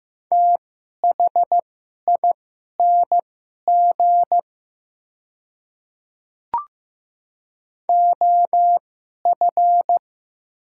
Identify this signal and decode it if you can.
Morse code, heard as a keyed tone, about 15 wpm, spaced as for 11 wpm, 700 Hz. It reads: THING OF
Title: 2x Character Spacing - Top 100 Words - 15wpm